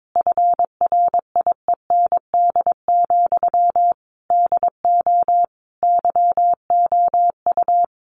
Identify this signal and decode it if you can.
Morse code, heard as a keyed tone, about 22 wpm, 700 Hz. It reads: FRIEND, DO YOU
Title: Sentences from Top 1100 Words 22wpm